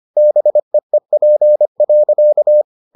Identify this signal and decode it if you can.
Morse code, heard as a keyed tone, about 25 wpm, 600 Hz. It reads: BEEP.